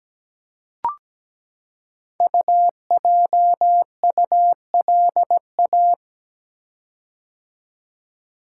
Morse code, heard as a keyed tone, about 17 wpm, 700 Hz. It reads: UJULA
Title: Code Groups - 5 Letters - 17wpm